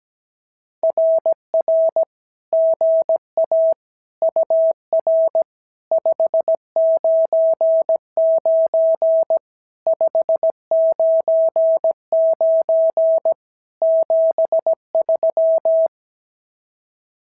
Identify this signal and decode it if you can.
Morse code, heard as a keyed tone, about 17 wpm, 650 Hz. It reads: RR GA UR 599 599 73
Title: SOTA QSO Practice 17wpm